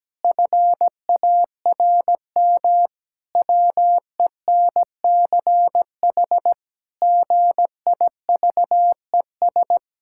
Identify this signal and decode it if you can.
Morse code, heard as a keyed tone, about 17 wpm, 700 Hz. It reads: FARM WENCH GIVES
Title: Pangram Challenge v2 - 17wpm